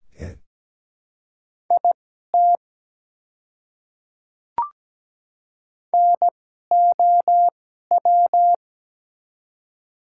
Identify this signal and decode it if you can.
Morse code, heard as a keyed tone, about 17 wpm, spaced as for 12 wpm, 700 Hz. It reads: IT NOW
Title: 2x Character Spacing - Top 200 Words - 17wpm